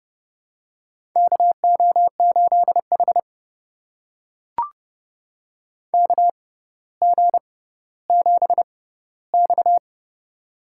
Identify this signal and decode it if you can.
Morse code, heard as a keyed tone, about 32 wpm, spaced as for 16 wpm, 700 Hz. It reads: KO8H KG7X
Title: USA Call Signs - 4 Characters - 6x Character Spacing - 30wpm